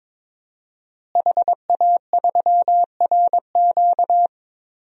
Morse code, heard as a keyed tone, about 22 wpm, 700 Hz. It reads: HA3RQ
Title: DXCC-100 Call Signs - 5 Characters - Repeated 2x - 22wpm